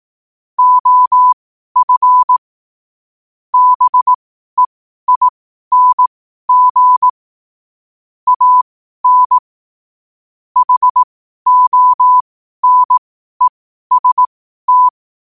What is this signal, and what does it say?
Morse code, heard as a keyed tone, about 18 wpm, spaced as for 12 wpm, 1000 Hz. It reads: OF BEING AN HONEST
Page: qotd-at-12-wpm